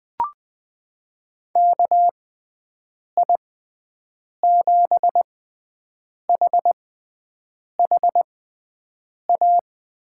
Morse code, heard as a keyed tone, about 20 wpm, spaced as for 7 wpm, 700 Hz. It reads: KI7HHA
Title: USA Call Signs - 6 Characters - 6x Character Spacing - 20wpm